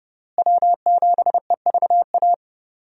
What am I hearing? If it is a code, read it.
Morse code, heard as a keyed tone, about 32 wpm, 700 Hz. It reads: W7EVA
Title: USA Call Signs - 5 Characters - Repeated 3x - 30wpm